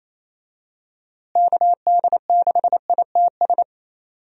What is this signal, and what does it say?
Morse code, heard as a keyed tone, about 28 wpm, 700 Hz. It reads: KD6ITS
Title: USA Call Signs - 6 Characters - Repeated 2x - 28wpm